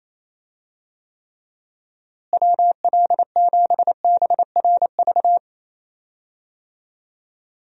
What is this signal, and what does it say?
Morse code, heard as a keyed tone, about 28 wpm, 700 Hz. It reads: WL7BRV
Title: USA Call Signs - 6 Characters - Repeated 2x - 28wpm